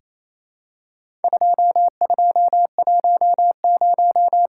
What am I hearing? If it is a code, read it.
Morse code, heard as a keyed tone, about 28 wpm, 700 Hz. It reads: 2210
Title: Worked All Europe DX Contest - QTC Exchange - 28wpm